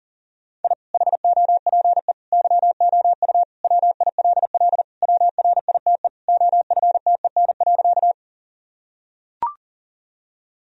Morse code, heard as a keyed tone, about 40 wpm, 700 Hz. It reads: I HOPE YOU WILL WRITE OFTEN.
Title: Sentences from Top 1000 Words - Review Part 1 - 40wpm